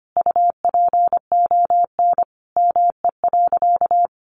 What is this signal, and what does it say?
Morse code, heard as a keyed tone, about 25 wpm, 700 Hz. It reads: UPON ME.
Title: Sentences from Top 1300 Words 25wpm